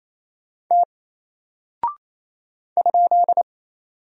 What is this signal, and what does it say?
Morse code, heard as a keyed tone, about 28 wpm, 700 Hz. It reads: T ?
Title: Single Character (Letter:Number:Punctuation:Prosign) Alt 28wpm